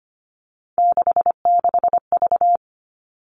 Morse code, heard as a keyed tone, about 25 wpm, 700 Hz. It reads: 66V
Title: All Cut Numbers - 3-Digit 25wpm